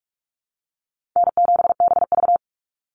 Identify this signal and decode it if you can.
Morse code, heard as a keyed tone, about 44 wpm, 700 Hz. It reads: N7BV
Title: Call Signs Repeated 2x - 45wpm